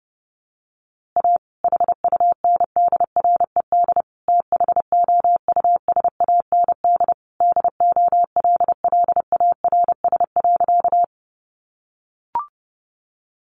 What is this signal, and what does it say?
Morse code, heard as a keyed tone, about 30 wpm, 700 Hz. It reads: A HUNDRED THOUSAND DOLLARS.
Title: A Sound Of Thunder 30wpm